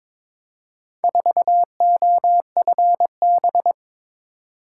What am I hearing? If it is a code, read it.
Morse code, heard as a keyed tone, about 22 wpm, 700 Hz. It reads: 4OFB
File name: practice / 22wpm / Four Letter-Number Combinations - 22wpm